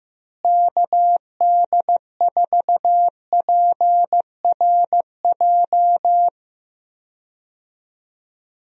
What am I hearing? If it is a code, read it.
Morse code, heard as a keyed tone, about 16 wpm, 700 Hz. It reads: KD4PRJ
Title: USA Call Signs - 6 Characters - Repeated 2x - 15wpm